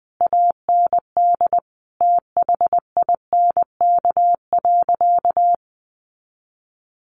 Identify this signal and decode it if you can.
Morse code, heard as a keyed tone, about 20 wpm, 700 Hz. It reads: AND THINK.
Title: Sentences from Top 500 Words 20wpm